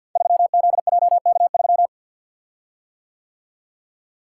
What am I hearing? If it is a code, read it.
Morse code, heard as a keyed tone, about 49 wpm, 690 Hz. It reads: 3ZJK3
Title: Code Groups - 5 Letter:Numbers - 50wpm